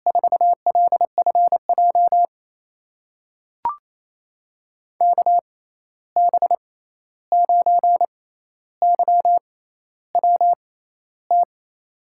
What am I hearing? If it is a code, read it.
Morse code, heard as a keyed tone, about 28 wpm, spaced as for 13 wpm, 700 Hz. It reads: 4LFJ KB9YWT
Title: USA Call Signs - 6 Characters - 6x Character Spacing - 28wpm